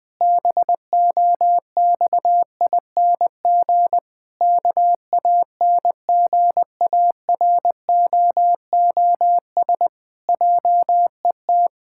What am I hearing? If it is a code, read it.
Morse code, heard as a keyed tone, about 20 wpm, 700 Hz. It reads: BOXING KANGAROOS JET
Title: Pangram Challenge v2 - 20wpm